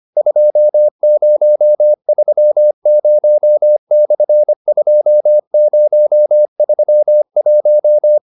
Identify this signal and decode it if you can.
Morse code, heard as a keyed tone, about 25 wpm, 600 Hz. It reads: 2030/2031